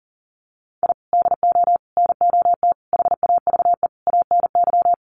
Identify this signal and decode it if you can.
Morse code, heard as a keyed tone, about 40 wpm, 700 Hz. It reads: I DO NOT HAVE ANY
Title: Sets of 5 Words 40wpm